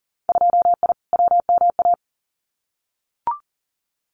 Morse code, heard as a keyed tone, about 40 wpm, 700 Hz. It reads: 2I WMA
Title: Winter Field Day Exchanges Encore 40wpm